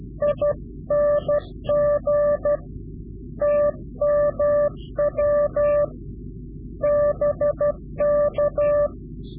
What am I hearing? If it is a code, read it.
Morse code, heard as a keyed tone, about 12 wpm, 590 Hz. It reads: ING TMW BK
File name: CW-QSO / Directional Call / Directional QSO